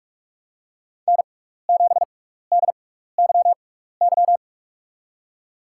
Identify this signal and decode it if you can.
Morse code, heard as a keyed tone, about 46 wpm, spaced as for 15 wpm, 700 Hz. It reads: N7DYY
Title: USA Call Signs - 5 Characters - 6x Character Spacing - 45wpm